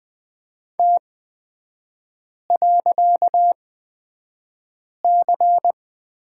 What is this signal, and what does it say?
Morse code, heard as a keyed tone, about 20 wpm, spaced as for 5 wpm, 700 Hz. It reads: T.C
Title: 255 Lesson 41e - Period - Three Characters 20wpm